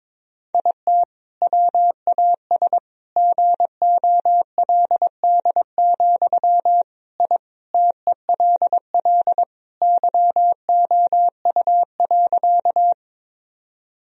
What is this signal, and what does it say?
Morse code, heard as a keyed tone, about 22 wpm, 700 Hz. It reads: IT WAS GOLD, I TELL YOU.
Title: Sentences from Top 1000 Words - Review Part 3 - 22wpm